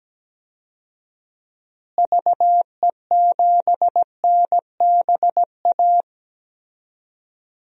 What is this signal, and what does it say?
Morse code, heard as a keyed tone, about 17 wpm, 700 Hz. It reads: VE7NBA
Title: DXCC-100 Call Signs - 6 Characters - Repeated 2x - 17wpm